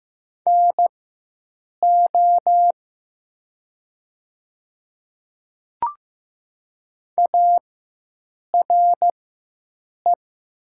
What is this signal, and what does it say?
Morse code, heard as a keyed tone, about 15 wpm, spaced as for 7 wpm, 700 Hz. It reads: NO ARE
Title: 4x Character Spacing - Top 100 Words - 15wpm